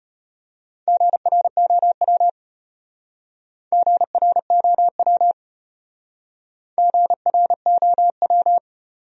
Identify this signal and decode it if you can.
Morse code, heard as a keyed tone, about 33 wpm, 700 Hz. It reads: GROW GROW GROW